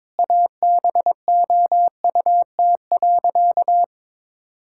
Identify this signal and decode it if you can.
Morse code, heard as a keyed tone, about 21 wpm, 700 Hz. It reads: ABOUT.
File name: practice / 22wpm / Sentences with 8-Word Phrases - 22wpm